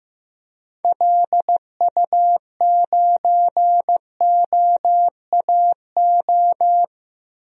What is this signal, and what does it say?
Morse code, heard as a keyed tone, about 15 wpm, 700 Hz. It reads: LU9OAO
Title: DXCC-100 Call Signs - 6 Characters - 15wpm